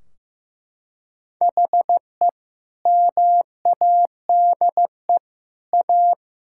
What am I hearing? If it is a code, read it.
Morse code, heard as a keyed tone, about 14 wpm, 700 Hz. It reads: HE MADE A
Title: Sentences from Top 1600 Words 15wpm